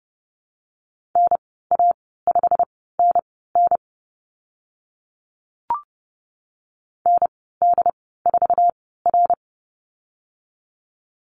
Morse code, heard as a keyed tone, about 30 wpm, spaced as for 17 wpm, 700 Hz. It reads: NA5NN ND4R
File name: Lesson 15g - Number 5 - Call Signs 30wpm